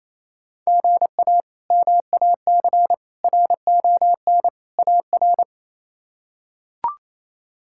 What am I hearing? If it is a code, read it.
Morse code, heard as a keyed tone, about 28 wpm, 700 Hz. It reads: GA MAC RON AR
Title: K1USN SST QSO Practice - 28wpm